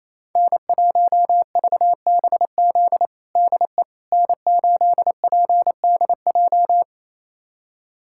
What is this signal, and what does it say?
Morse code, heard as a keyed tone, about 28 wpm, 700 Hz. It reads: N1VBZ DE N8PDJ
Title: USA Callsign DE Callsign - 5 Characters - 28wpm